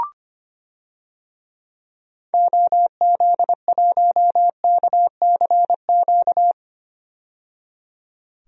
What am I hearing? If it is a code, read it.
Morse code, heard as a keyed tone, about 25 wpm, 700 Hz. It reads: OZ1KCQ
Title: DXCC-100 Call Signs - 6 Characters - 25wpm